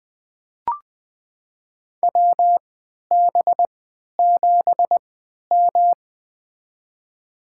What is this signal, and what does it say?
Morse code, heard as a keyed tone, about 20 wpm, spaced as for 11 wpm, 700 Hz. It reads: WB7M
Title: USA Call Signs - 4 Characters - 3x Character Spacing - 20wpm